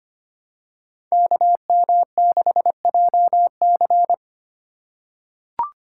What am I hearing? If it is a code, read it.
Morse code, heard as a keyed tone, about 25 wpm, 700 Hz. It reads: KM6JC
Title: Code Groups - 5 Letter:Numbers - 25wpm